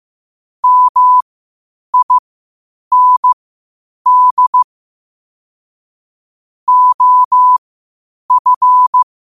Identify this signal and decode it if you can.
Morse code, heard as a keyed tone, about 15 wpm, spaced as for 8 wpm, 1000 Hz. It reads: MIND OF
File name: qotdfiles / 8wpm